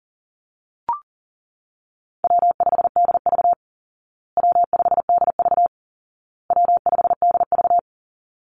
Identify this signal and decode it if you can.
Morse code, heard as a keyed tone, about 40 wpm, 700 Hz. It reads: W5DV W5DV W5DV
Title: Call Signs Repeated 2x - 40wpm